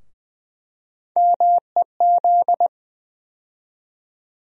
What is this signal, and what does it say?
Morse code, heard as a keyed tone, about 21 wpm, 700 Hz. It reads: MEZ
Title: USA Call Sign Postfixes - 3 Letters 20wpm